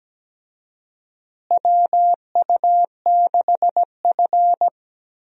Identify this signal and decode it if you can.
Morse code, heard as a keyed tone, about 17 wpm, 700 Hz. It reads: WU6F
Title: USA Call Signs - 4 Characters - Repeated 2x - 17wpm